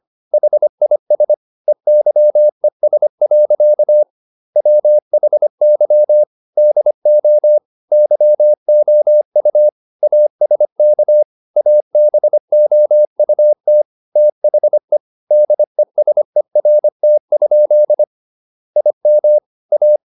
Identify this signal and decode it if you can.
Morse code, heard as a keyed tone, about 25 wpm, 600 Hz. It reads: HIS EYES. WHY DO YOU ASK ABOUT THE DESERT? IM A